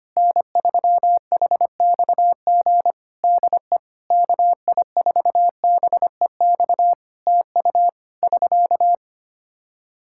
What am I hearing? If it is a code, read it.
Morse code, heard as a keyed tone, about 25 wpm, 700 Hz. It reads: N3HXG DE KI4BEX TU <SK>
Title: K1USN SST QSO Practice 25wpm